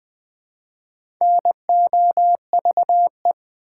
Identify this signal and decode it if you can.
Morse code, heard as a keyed tone, about 20 wpm, 700 Hz. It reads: NOVE
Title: Build A Word - 1 Letter At A Time - 5-Letter Words - 20wpm